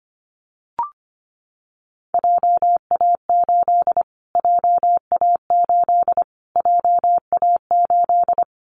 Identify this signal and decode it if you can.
Morse code, heard as a keyed tone, about 25 wpm, 700 Hz. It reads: JA8 JA8 JA8
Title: Callsign Prefixes for Top 25 DXCC - Repeated 3x - 25wpm